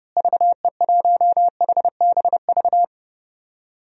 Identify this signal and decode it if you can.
Morse code, heard as a keyed tone, about 30 wpm, 700 Hz. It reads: VE1HBV
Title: DXCC-100 Call Signs - 6 Characters - Repeated 3x - 30wpm